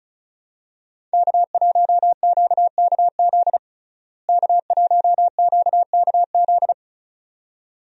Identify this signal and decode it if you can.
Morse code, heard as a keyed tone, about 34 wpm, 700 Hz. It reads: K1QKZ K1QKZ
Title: USA Call Signs - 5 Characters - Repeated 2x - 35wpm